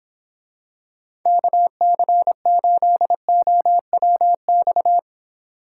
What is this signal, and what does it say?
Morse code, heard as a keyed tone, about 26 wpm, 700 Hz. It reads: KC8OWX